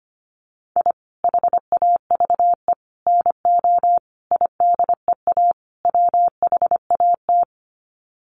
Morse code, heard as a keyed tone, about 25 wpm, 700 Hz. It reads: I HAVE NO IDEA WHAT
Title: Sets of 5 Words Encore 25wpm